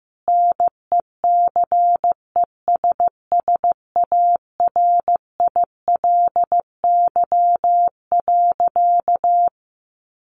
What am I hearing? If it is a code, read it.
Morse code, heard as a keyed tone, about 15 wpm, 700 Hz. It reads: NECESSARILY.
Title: Sentences from Top 1400 Words 15wpm